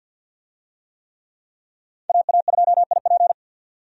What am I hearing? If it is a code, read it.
Morse code, heard as a keyed tone, about 52 wpm, 700 Hz. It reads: AA2IP
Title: USA Call Signs - 5 Characters - 50wpm